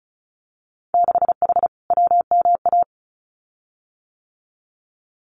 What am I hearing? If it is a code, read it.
Morse code, heard as a keyed tone, about 36 wpm, 700 Hz. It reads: 6H WMA
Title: Winter Field Day Exchanges Encore 35wpm